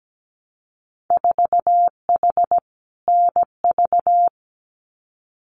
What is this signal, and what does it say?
Morse code, heard as a keyed tone, about 17 wpm, 700 Hz. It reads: 4H NV
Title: Winter Field Day Exchanges Encore 17wpm